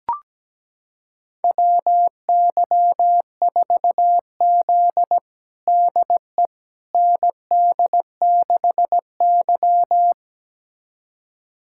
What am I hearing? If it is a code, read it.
Morse code, heard as a keyed tone, about 17 wpm, 700 Hz. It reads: WY4Z DE ND6Y
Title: USA Callsign DE Callsign - 4 Characters - 17wpm